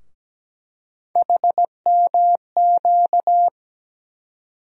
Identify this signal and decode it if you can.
Morse code, heard as a keyed tone, about 17 wpm, 700 Hz. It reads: HMQ
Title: USA Call Sign Postfixes - 3 Letters 17wpm